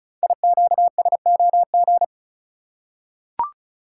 Morse code, heard as a keyed tone, about 36 wpm, 690 Hz. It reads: IQSOG
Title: Code Groups - 5 Letters - 35wpm